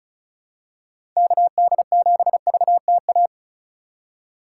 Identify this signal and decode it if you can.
Morse code, heard as a keyed tone, about 34 wpm, 700 Hz. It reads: KD7VTA
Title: USA Call Signs - 3x Character Spacing - 35wpm